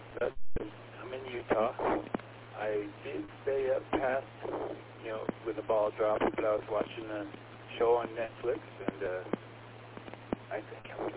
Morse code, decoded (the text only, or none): none